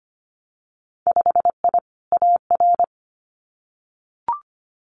Morse code, heard as a keyed tone, about 25 wpm, 700 Hz. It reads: 5I AR
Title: Winter Field Day Exchanges Encore 25wpm